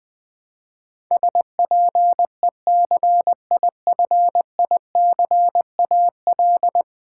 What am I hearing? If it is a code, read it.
Morse code, heard as a keyed tone, about 21 wpm, 700 Hz. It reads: SPECIFICAL
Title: Build A Word - 5 Letters At A Time - 12-Letter Words - 20wpm